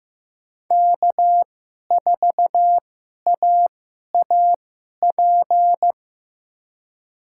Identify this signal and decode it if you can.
Morse code, heard as a keyed tone, about 15 wpm, spaced as for 11 wpm, 700 Hz. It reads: K4AAP
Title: USA Call Signs - 5 Characters - 2x Character Spacing - 15wpm